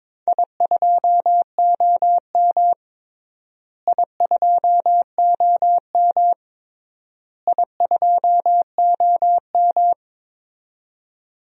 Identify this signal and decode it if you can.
Morse code, heard as a keyed tone, about 22 wpm, 700 Hz. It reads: I2OM I2OM I2OM
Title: DXCC-100 Call Signs - 4 Characters - Repeated 3x - 22wpm